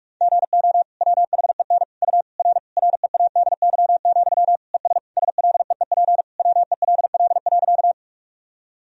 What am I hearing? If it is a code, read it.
Morse code, heard as a keyed tone, about 46 wpm, 700 Hz. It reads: GO WHEN U R READY, ES SLEEP WELL.
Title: Sentences from Top 500 Words and Abbreviations 45wpm